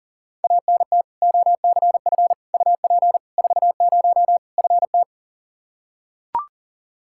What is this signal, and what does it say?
Morse code, heard as a keyed tone, about 37 wpm, 700 Hz. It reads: ANT OCF UP 40 FT
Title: K1USN SST QSO Practice 40wpm